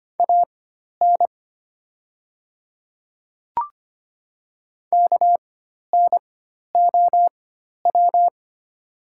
Morse code, heard as a keyed tone, about 25 wpm, spaced as for 12 wpm, 700 Hz. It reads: AN KNOW
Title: 4x Character Spacing - Top 100 Words - 25wpm